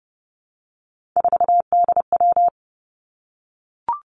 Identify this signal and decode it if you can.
Morse code, heard as a keyed tone, about 30 wpm, 700 Hz. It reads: 4DW
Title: Three Challenging Letter-Number Combinations 30wpm